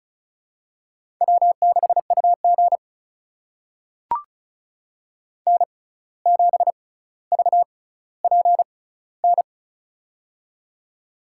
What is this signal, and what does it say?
Morse code, heard as a keyed tone, about 36 wpm, spaced as for 17 wpm, 700 Hz. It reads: W6UG N7VPN
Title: USA Call Signs - 6x Character Spacing - 35wpm